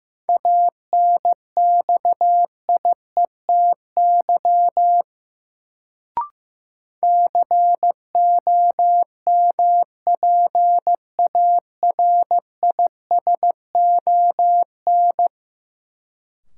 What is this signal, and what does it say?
Morse code, heard as a keyed tone, about 15 wpm, 700 Hz. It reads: ANXIETY COMPARISON
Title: Sentences from Top 1600 Words - New Word Drill Alt - 15wpm